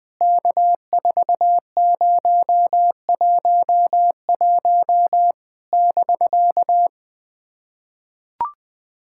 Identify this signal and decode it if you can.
Morse code, heard as a keyed tone, about 18 wpm, 700 Hz. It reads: K4011 <BK>
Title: POTA QSO Practice - 20wpm